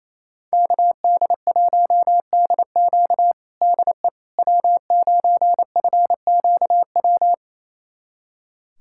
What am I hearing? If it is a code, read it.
Morse code, heard as a keyed tone, about 28 wpm, 700 Hz. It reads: KD1DQ DE W9FQW
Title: USA Callsign DE Callsign - 5 Characters - 28wpm